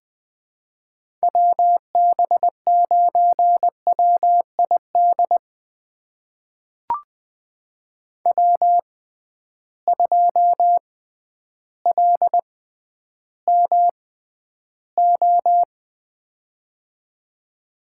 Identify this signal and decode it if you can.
Morse code, heard as a keyed tone, about 20 wpm, spaced as for 11 wpm, 700 Hz. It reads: WB9WID W2LMO
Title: USA Call Signs - 6x Character Spacing - 20wpm